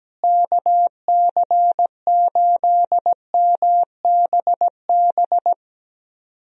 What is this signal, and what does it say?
Morse code, heard as a keyed tone, about 17 wpm, 700 Hz. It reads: KC8MBB
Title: USA Call Signs - 6 Characters - Repeated 2x - 17wpm